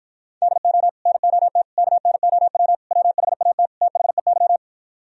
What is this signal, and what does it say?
Morse code, heard as a keyed tone, about 47 wpm, 700 Hz. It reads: DO NOT KNOW WHAT THEY